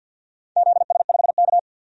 Morse code, heard as a keyed tone, about 51 wpm, 700 Hz. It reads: ZS5K
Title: DXCC-100 Call Signs - 4 Characters - 50wpm